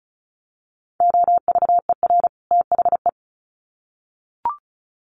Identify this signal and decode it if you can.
Morse code, heard as a keyed tone, about 34 wpm, 700 Hz. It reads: OVER THE
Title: Sets of 2 Words 35wpm